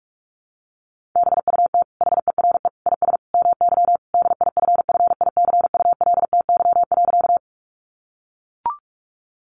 Morse code, heard as a keyed tone, about 44 wpm, 700 Hz. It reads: BUT HERE IS MY DIFFICULTY.
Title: Sentences from Top 1500 Words 45wpm